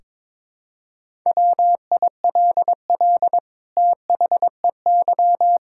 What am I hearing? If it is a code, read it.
Morse code, heard as a keyed tone, about 22 wpm, 700 Hz. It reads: WILL THEY
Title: Sentences from Top 1600 Words 22wpm